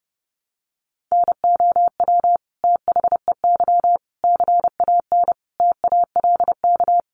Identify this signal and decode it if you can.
Morse code, heard as a keyed tone, about 30 wpm, 700 Hz. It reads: NOW THEY CAN TALK
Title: Sentences from Top 700 Words 30wpm